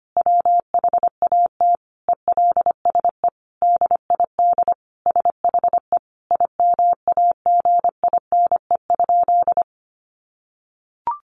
Morse code, heard as a keyed tone, about 25 wpm, 700 Hz. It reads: WHAT ELSE DID SHE IMAGINE?
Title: Sentences from Top 1400 Words 25wpm